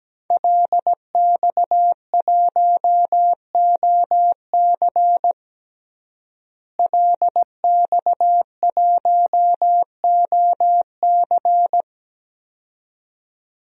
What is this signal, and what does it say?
Morse code, heard as a keyed tone, about 17 wpm, 700 Hz. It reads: LX1OC LX1OC
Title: DXCC-100 Call Signs - 5 Characters - Repeated 2x - 17wpm